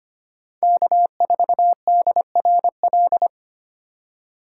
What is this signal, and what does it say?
Morse code, heard as a keyed tone, about 32 wpm, 700 Hz. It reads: K4DRL